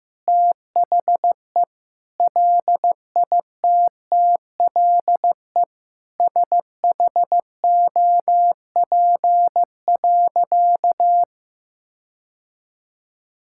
Morse code, heard as a keyed tone, about 15 wpm, 700 Hz. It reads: THE LITTLE SHOP.
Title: Sentences from Top 1600 Words 15wpm